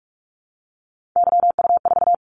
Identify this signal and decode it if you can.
Morse code, heard as a keyed tone, about 46 wpm, 700 Hz. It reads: YU4
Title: Callsign Prefixes for Top 25 DXCC - 45wpm